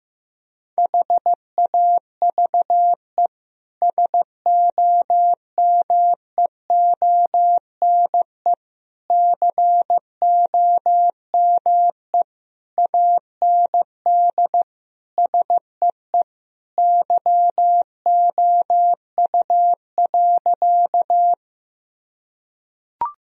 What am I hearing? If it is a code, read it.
Morse code, heard as a keyed tone, about 15 wpm, 700 Hz. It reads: HAVE SOMEONE COME AND SEE YOU.
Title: Sentences from Top 1000 Words - Review Part 4 - 15wpm